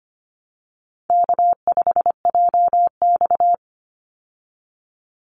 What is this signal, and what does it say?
Morse code, heard as a keyed tone, about 25 wpm, 700 Hz. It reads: K5JX
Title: Call Signs Repeated 2x - 25wpm